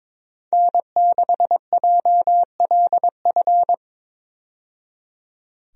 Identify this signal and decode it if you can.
Morse code, heard as a keyed tone, about 22 wpm, 700 Hz. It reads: N6JLF
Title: USA Call Signs - 5 Characters - 22wpm